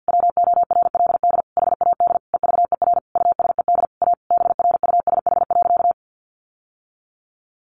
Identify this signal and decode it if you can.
Morse code, heard as a keyed tone, about 50 wpm, 700 Hz. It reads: WORLD HAD EVER USED A BRUSH.
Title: Sentences from Top 1100 Words 50wpm